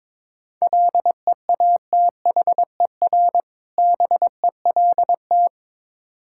Morse code, heard as a keyed tone, about 22 wpm, 690 Hz. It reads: LEATHER BELT
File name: Difficult Words with Hints - 22wpm